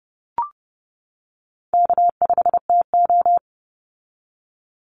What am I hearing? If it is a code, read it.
Morse code, heard as a keyed tone, about 30 wpm, 700 Hz. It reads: K5TO
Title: Easy Call Signs - 30wpm